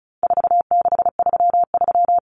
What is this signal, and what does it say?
Morse code, heard as a keyed tone, about 34 wpm, 700 Hz. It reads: 4633
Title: Common Cut Numbers - 4-Digit 35wpm